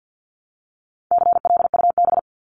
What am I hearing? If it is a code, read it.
Morse code, heard as a keyed tone, about 50 wpm, 700 Hz. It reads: CLUB
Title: Top 1001-1250 Words 50wpm